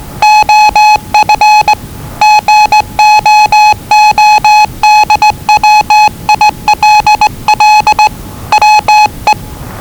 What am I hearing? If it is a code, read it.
Morse code, heard as a keyed tone, about 20 wpm, 850 Hz. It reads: OF GOODWILL WE
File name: WellingtonRadioZlwCloseDownOnHfAndMf / 01-Zlw 4277khz 417.5 qrt until 0925z